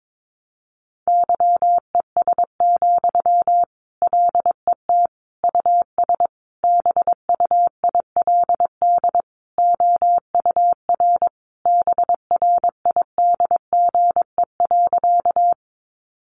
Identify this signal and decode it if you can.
Morse code, heard as a keyed tone, about 22 wpm, 700 Hz. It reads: YES, LET US BUILD OUR BRIDGE.
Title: Sentences from Top 1200 Words 22wpm